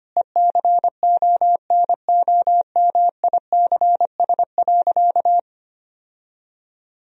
Morse code, heard as a keyed tone, about 25 wpm, 700 Hz. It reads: ECONOMICS.
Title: Sentences from Top 1000 Words - Review Part 3 - 25wpm